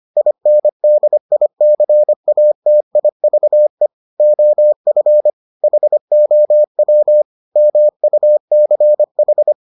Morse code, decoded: INDICATIVE OF HOW MUCH

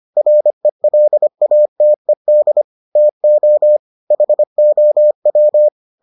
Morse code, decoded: RELATED TO HOW

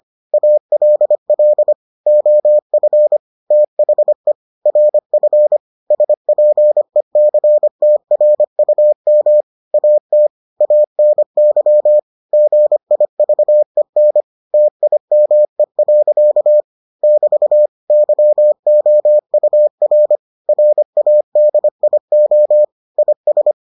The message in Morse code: ALL OF THE RF SPECTRUM AT ANY GIVEN TIME. <BT> YOUR RADIO IS

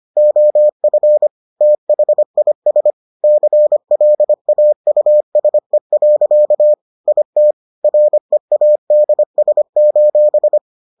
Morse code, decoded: OF THIS CLAUSE. IT READS:

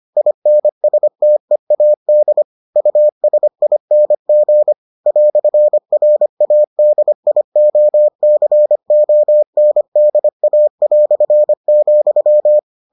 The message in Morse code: INSTEAD USING "RADIOCONDA",

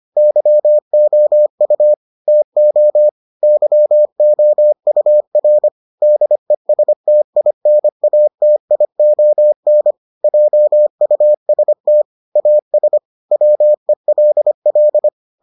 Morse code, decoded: YOU TO YOUR DESTINATION JUST AS WELL

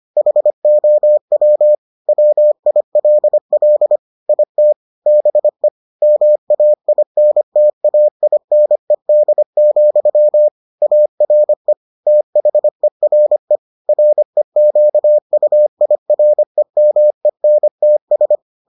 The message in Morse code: HOW WILL IT BE MAINTAINED, ARE THERE REQUIREMENTS

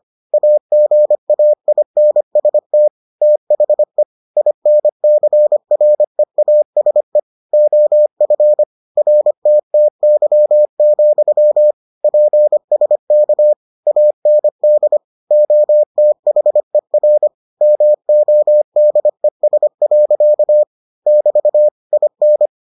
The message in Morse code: AGAINST THE INCREASE OF RTTY, PSK AND OTHER MODES. <BT> IN

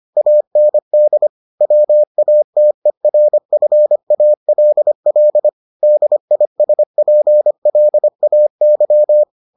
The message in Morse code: AND WATERFALL DISPLAY

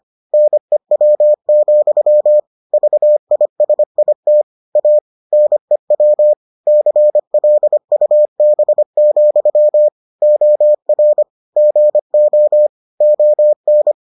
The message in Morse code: NEW, VISIT A NEW CLUB, OR GO ON

